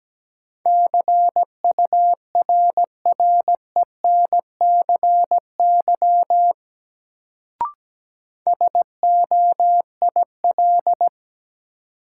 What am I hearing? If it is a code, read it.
Morse code, heard as a keyed tone, about 17 wpm, 700 Hz. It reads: CURRENCY SOIL
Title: Sentences from Top 1500 Words - New Word Drill Alt - 17wpm